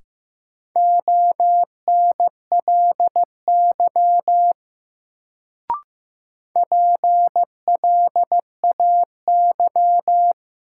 Morse code, heard as a keyed tone, about 15 wpm, 700 Hz. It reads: ONLY PLAY
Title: Sentences from Top 200 Words - New Word Drill - 15wpm